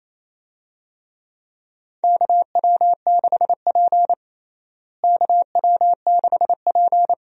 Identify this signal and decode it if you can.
Morse code, heard as a keyed tone, about 28 wpm, 700 Hz. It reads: KW6P KW6P
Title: USA Call Signs - 4 Characters - Repeated 2x - 28wpm